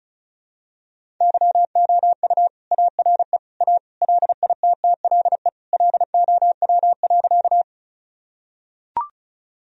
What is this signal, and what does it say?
Morse code, heard as a keyed tone, about 36 wpm, 700 Hz. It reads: YOU ARE A LITTLE LOW.
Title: Sentences from Top 1000 Words - Review Part 1 - 35wpm